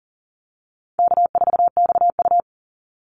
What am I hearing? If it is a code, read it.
Morse code, heard as a keyed tone, about 40 wpm, 700 Hz. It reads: K4XU
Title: Easy Call Signs Repeated 2x - 40wpm